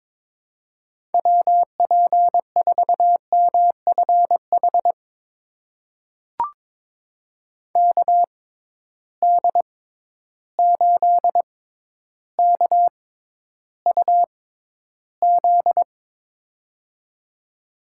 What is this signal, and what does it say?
Morse code, heard as a keyed tone, about 22 wpm, spaced as for 11 wpm, 700 Hz. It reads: WP4MFH KD8KUZ
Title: USA Call Signs - 6 Characters - 6x Character Spacing - 22wpm